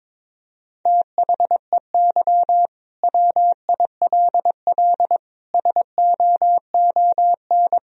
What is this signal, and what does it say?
Morse code, heard as a keyed tone, about 22 wpm, 700 Hz. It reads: THEY WILL SOON
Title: Sentences from Top 1000 Words - Review Part 4 - 22wpm